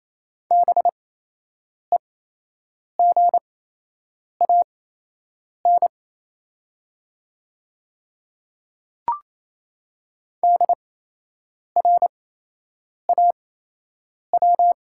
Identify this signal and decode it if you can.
Morse code, heard as a keyed tone, about 27 wpm, spaced as for 8 wpm, 690 Hz. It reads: BEGAN DRAW
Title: 8x Character Spacing - Top 300 Words - 28wpm